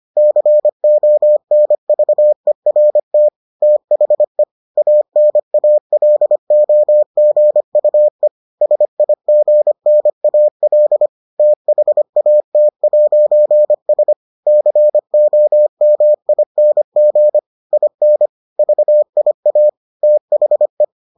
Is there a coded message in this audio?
Yes: Morse code from 0.2 to 20.8 s, heard as a keyed tone, about 25 wpm, 600 Hz. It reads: CONVERT THE ANALOGUE SIGNAL THAT'S COMING IN VIA THE